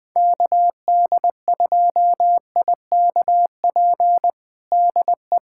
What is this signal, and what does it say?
Morse code, heard as a keyed tone, about 20 wpm, 700 Hz. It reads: KD2IKP DE
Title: USA Callsign DE Callsign - 6 Characters - 20wpm